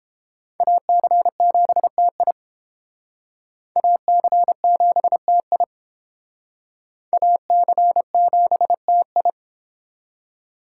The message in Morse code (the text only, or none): AC7TI AC7TI AC7TI